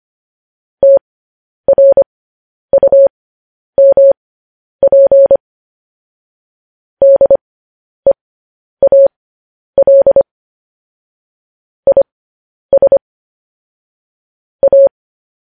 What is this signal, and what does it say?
Morse code, heard as a keyed tone, about 25 wpm, spaced as for 10 wpm, 550 Hz. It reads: TRUMP DEAL IS A
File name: podcast / News headlines-1x-25x10